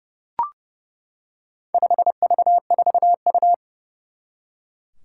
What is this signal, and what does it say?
Morse code, heard as a keyed tone, about 30 wpm, 700 Hz. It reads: 5V4U
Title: Four Challenging Letter-Number Combinations 30wpm